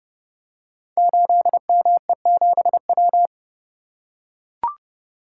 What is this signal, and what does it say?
Morse code, heard as a keyed tone, about 30 wpm, 700 Hz. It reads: 8ME7W
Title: Code Groups - 5 Letter:Numbers - 30wpm